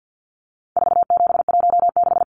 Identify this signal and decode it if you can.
Morse code, heard as a keyed tone, about 48 wpm, 700 Hz. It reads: 4716
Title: Numbers - 4-Digit 50wpm